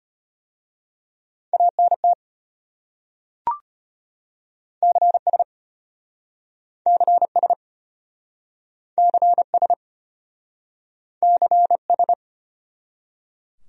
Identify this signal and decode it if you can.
Morse code, heard as a keyed tone, about 33 wpm, 700 Hz. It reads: ANT CS CS CS CS